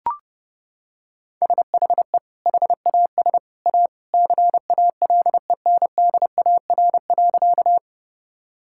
Morse code, heard as a keyed tone, about 30 wpm, 700 Hz. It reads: SHE HAS A CALENDAR.
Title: Sentences from Top 1600 Words 30wpm